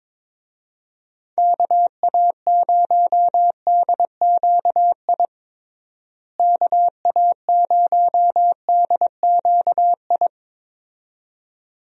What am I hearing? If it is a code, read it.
Morse code, heard as a keyed tone, about 22 wpm, 700 Hz. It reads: KA0DQI KA0DQI
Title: USA Call Signs - 6 Characters - Repeated 2x - 22wpm